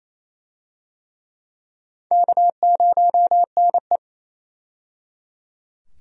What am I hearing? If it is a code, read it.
Morse code, heard as a keyed tone, about 28 wpm, 700 Hz. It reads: K0NE
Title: USA Call Signs - 4 Characters - 28wpm